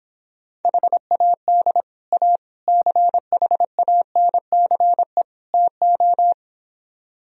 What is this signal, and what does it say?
Morse code, heard as a keyed tone, about 26 wpm, 700 Hz. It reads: HAD A CHANCE TO